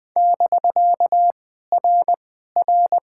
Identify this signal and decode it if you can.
Morse code, heard as a keyed tone, about 20 wpm, 700 Hz. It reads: <BK> R R
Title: POTA QSO Practice - 20wpm